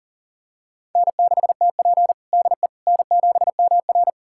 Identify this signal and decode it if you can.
Morse code, heard as a keyed tone, about 37 wpm, 700 Hz. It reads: N6TP DE N7MR
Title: USA Callsign DE Callsign - 4 Characters - 40wpm